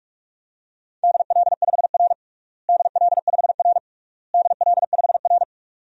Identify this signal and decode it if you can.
Morse code, heard as a keyed tone, about 42 wpm, 700 Hz. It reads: DL5R DL5R DL5R
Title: DXCC-100 Call Signs - 4 Characters - Repeated 3x - 45wpm